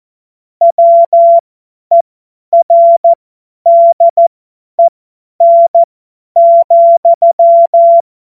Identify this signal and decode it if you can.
Morse code, heard as a keyed tone, about 14 wpm, spaced as for 10 wpm, 680 Hz. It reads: WERDEN,
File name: Morsekurs / Witze / Witz 28. September 2021, 50 F70